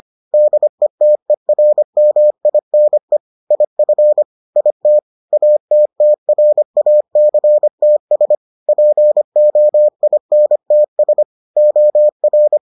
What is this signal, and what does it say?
Morse code, heard as a keyed tone, about 25 wpm, 600 Hz. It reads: DETERMINE IF IT ATTRACTS POINTS OR